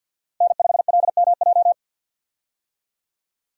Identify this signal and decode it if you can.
Morse code, heard as a keyed tone, about 50 wpm, 700 Hz. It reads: N5LMJ
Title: USA Call Signs - 5 Characters - Repeated 3x - 50wpm